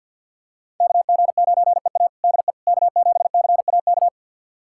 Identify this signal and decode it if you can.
Morse code, heard as a keyed tone, about 46 wpm, 700 Hz. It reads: KG9EA DE K7CAK
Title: USA Callsign DE Callsign - 5 Characters - 50wpm